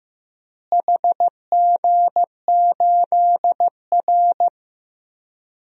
Morse code, heard as a keyed tone, about 15 wpm, 700 Hz. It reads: HG8R
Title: DXCC-100 Call Signs - 4 Characters - Repeated 2x - 15wpm